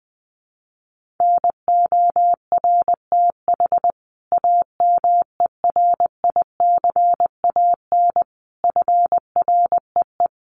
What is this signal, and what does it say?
Morse code, heard as a keyed tone, about 20 wpm, 700 Hz. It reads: NORTH AMERICAN FREE
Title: Sets of 5 Words Encore 20wpm